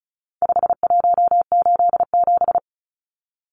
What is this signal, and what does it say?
Morse code, heard as a keyed tone, about 34 wpm, 700 Hz. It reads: 5187
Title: Numbers - 4-Digit 35wpm